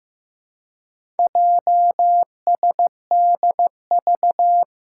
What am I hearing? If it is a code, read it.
Morse code, heard as a keyed tone, about 15 wpm, 700 Hz. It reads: JSDV
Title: Four Challenging Letter-Number Combinations 15wpm